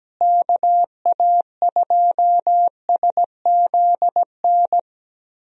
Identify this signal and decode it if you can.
Morse code, heard as a keyed tone, about 17 wpm, 700 Hz. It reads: KA2SZN
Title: USA Call Signs - 6 Characters - Repeated 3x - 17wpm